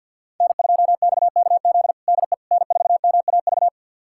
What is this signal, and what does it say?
Morse code, heard as a keyed tone, about 52 wpm, 700 Hz. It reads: N2XKZ DE N4MAV
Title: USA Callsign DE Callsign - 5 Characters - 50wpm